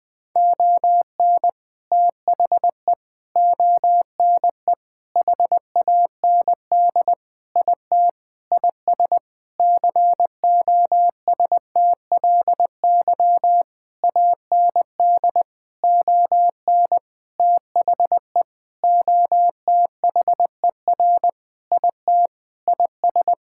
Morse code, heard as a keyed tone, about 21 wpm, 700 Hz. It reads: ON THE ONE HAND IT IS COSTLY AND ON THE OTHER IT IS
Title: Sentences with 8-Word Phrases - 20wpm